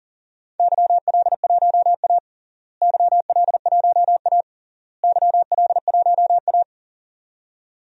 Morse code, heard as a keyed tone, about 38 wpm, 700 Hz. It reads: YL1A YL1A YL1A
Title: DXCC-100 Call Signs - 4 Characters - Repeated 3x - 40wpm